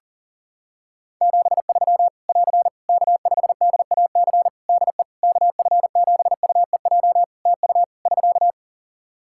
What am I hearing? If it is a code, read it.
Morse code, heard as a keyed tone, about 40 wpm, 700 Hz. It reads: 73 <AR> K5DAC DE KF7UEJ TU <SK>
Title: Practice QSO - Exchange 4 - Ending - 40wpm